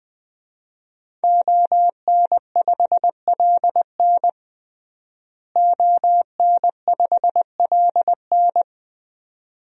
Morse code, heard as a keyed tone, about 21 wpm, 700 Hz. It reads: ON5LN ON5LN
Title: DXCC-100 Call Signs - 5 Characters - Repeated 2x - 20wpm